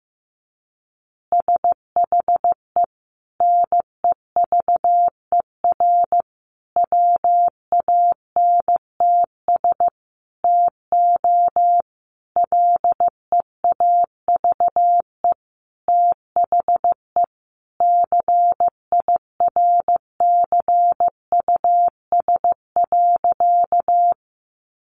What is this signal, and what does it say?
Morse code, heard as a keyed tone, about 15 wpm, 700 Hz. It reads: SHE NEVER WANTS TO LEAVE THE CIRCUS.
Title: Easy Phrases 15wpm